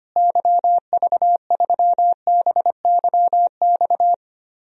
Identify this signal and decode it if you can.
Morse code, heard as a keyed tone, about 25 wpm, 700 Hz. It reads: YV3BYX
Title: DXCC-100 Call Signs - 6 Characters - Repeated 2x - 25wpm